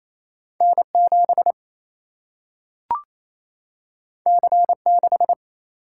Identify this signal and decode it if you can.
Morse code, heard as a keyed tone, about 28 wpm, 700 Hz. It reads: N 7 C 6
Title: Single-Letter Single-Number 28wpm